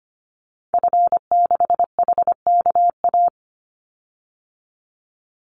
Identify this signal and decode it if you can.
Morse code, heard as a keyed tone, about 25 wpm, 700 Hz. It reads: F6HKA
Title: Call Signs Repeated 2x - 25wpm